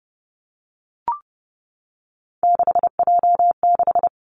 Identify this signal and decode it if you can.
Morse code, heard as a keyed tone, about 30 wpm, 700 Hz. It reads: 6J6
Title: Three Challenging Letter-Number Combinations 30wpm